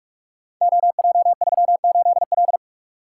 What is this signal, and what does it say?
Morse code, heard as a keyed tone, about 44 wpm, 700 Hz. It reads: OJ38L
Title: Code Groups - 5 Letter:Numbers - 45wpm